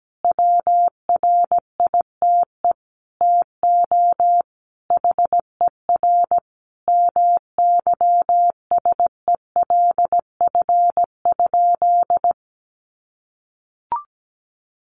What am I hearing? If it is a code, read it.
Morse code, heard as a keyed tone, about 17 wpm, 700 Hz. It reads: WRITE TO HER MYSELF?
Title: Sentences from Top 1300 Words 17wpm